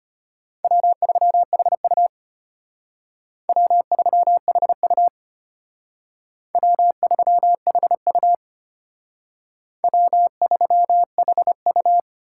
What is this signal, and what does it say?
Morse code, heard as a keyed tone, about 32 wpm, 700 Hz. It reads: W3HU W3HU W3HU W3HU